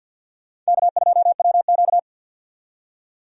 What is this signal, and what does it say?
Morse code, heard as a keyed tone, about 47 wpm, 700 Hz. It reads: K2WQ
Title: USA Call Signs - 4 Characters - 50wpm